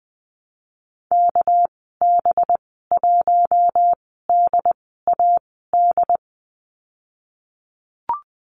Morse code, heard as a keyed tone, about 20 wpm, spaced as for 14 wpm, 700 Hz. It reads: KB1DAD
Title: 231 Lesson 37g - Letter x - Call Signs 20wpm